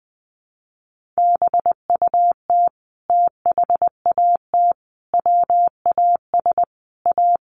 Morse code, heard as a keyed tone, about 20 wpm, 700 Hz. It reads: BUT THAT WAS A
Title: Sentences from Top 1100 Words 20wpm